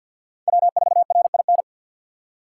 Morse code, heard as a keyed tone, about 63 wpm, 700 Hz. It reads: W4RIN